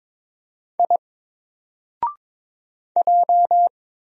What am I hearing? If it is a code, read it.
Morse code, heard as a keyed tone, about 22 wpm, 700 Hz. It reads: I J
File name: Single-Letter-Number-22wpm